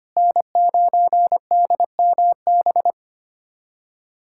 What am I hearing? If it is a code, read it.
Morse code, heard as a keyed tone, about 25 wpm, 700 Hz. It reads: N9DMB
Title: USA Callsign DE Callsign - 5 Characters - 25wpm